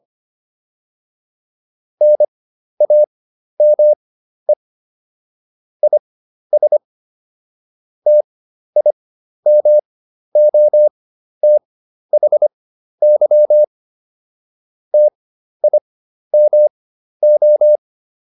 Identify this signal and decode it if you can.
Morse code, heard as a keyed tone, about 25 wpm, spaced as for 12 wpm, 600 Hz. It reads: NAME IS TIMOTHY TIMO